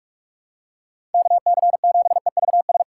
Morse code, heard as a keyed tone, about 47 wpm, 700 Hz. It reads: KC7EVS
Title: USA Call Signs - 2x Character Spacing - 45wpm